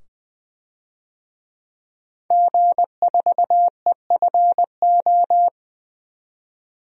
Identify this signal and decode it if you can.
Morse code, heard as a keyed tone, about 20 wpm, 700 Hz. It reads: G4EFO
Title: DXCC-100 Call Signs - 5 Characters - 20wpm